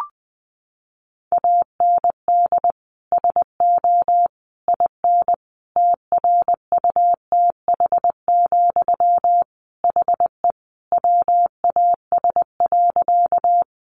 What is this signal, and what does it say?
Morse code, heard as a keyed tone, about 20 wpm, 700 Hz. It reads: AND SO IN TRUTH, HE WAS.
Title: Sentences from Top 800 Words 20wpm